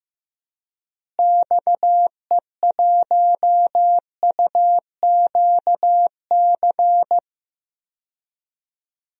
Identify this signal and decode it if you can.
Morse code, heard as a keyed tone, about 16 wpm, 700 Hz. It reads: XE1UQC
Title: DXCC-100 Call Signs - 6 Characters - Repeated 3x - 15wpm